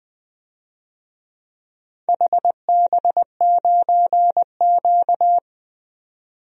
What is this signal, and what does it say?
Morse code, heard as a keyed tone, about 20 wpm, 700 Hz. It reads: HB9Q
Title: DXCC-100 Call Signs - 4 Characters - 20wpm